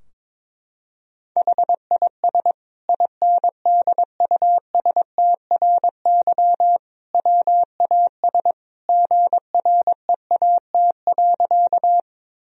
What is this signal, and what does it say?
Morse code, heard as a keyed tone, about 22 wpm, 700 Hz. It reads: HIS INDUSTRY WAS GREAT.
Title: Sentences from Top 1000 Words - Review Part 4 - 22wpm